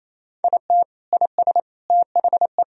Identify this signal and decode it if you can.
Morse code, heard as a keyed tone, about 28 wpm, 700 Hz. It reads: IT IS THE
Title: Sentences from Top 1000 Words - Review Part 4 - 28wpm